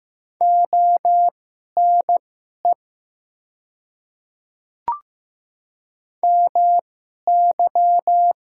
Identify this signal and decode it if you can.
Morse code, heard as a keyed tone, about 15 wpm, spaced as for 11 wpm, 700 Hz. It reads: ONE MY
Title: 2x Character Spacing - Top 200 Words - 15wpm